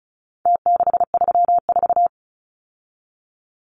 Morse code, heard as a keyed tone, about 34 wpm, 700 Hz. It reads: T634
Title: Numbers - Contest Style - 4-Digit 35wpm